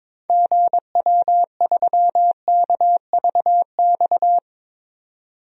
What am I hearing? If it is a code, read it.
Morse code, heard as a keyed tone, about 22 wpm, 700 Hz. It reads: GW3KVX
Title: DXCC-100 Call Signs - 6 Characters - 22wpm